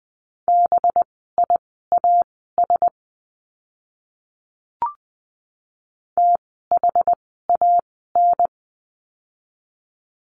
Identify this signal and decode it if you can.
Morse code, heard as a keyed tone, about 20 wpm, spaced as for 14 wpm, 700 Hz. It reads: BIAS THAN
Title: 185 Lesson 30f - Letter b - Words 20wpm